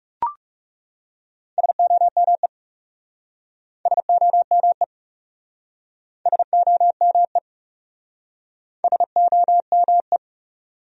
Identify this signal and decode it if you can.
Morse code, heard as a keyed tone, about 40 wpm, 700 Hz. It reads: SOME SOME SOME SOME